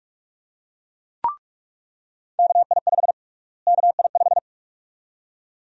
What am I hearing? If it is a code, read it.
Morse code, heard as a keyed tone, about 42 wpm, 700 Hz. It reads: KI5 KI5
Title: USA Call Sign Prefixes - Repeated 2x - 45wpm